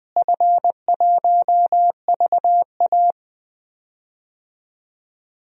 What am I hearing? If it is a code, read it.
Morse code, heard as a keyed tone, about 20 wpm, 700 Hz. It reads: F1VA
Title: DXCC-100 Call Signs - 4 Characters - Repeated 3x - 20wpm